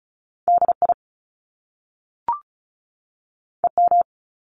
Morse code, heard as a keyed tone, about 36 wpm, 700 Hz. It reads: DI EM
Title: Top Two Letter Combinations in Words 35wpm